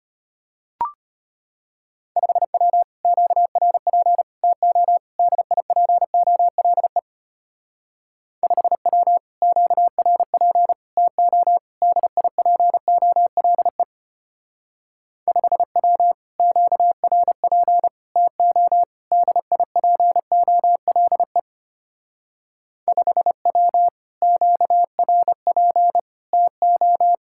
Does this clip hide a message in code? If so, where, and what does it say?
Yes: Morse code from 2.2 to 27.1 s, heard as a keyed tone, about 32 wpm, 700 Hz. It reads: 5W QRP TO DIPOLE 5W QRP TO DIPOLE 5W QRP TO DIPOLE 5W QRP TO